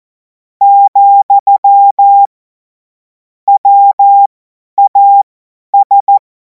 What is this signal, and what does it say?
Morse code, heard as a keyed tone, about 14 wpm, spaced as for 10 wpm, 800 Hz. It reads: , WAS
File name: Morsekurs / Witze / Witz 30. Juni 2026 50 F70